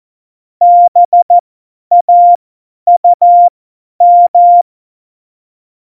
Morse code, heard as a keyed tone, about 14 wpm, spaced as for 10 wpm, 700 Hz. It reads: BAUM
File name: Morsekurs / Witze / Witz 05. Juni 2018, 50 F70